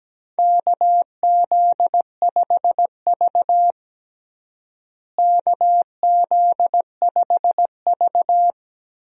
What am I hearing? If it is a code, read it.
Morse code, heard as a keyed tone, about 17 wpm, 700 Hz. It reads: KZ5V KZ5V
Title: USA Call Signs - 4 Characters - Repeated 3x - 17wpm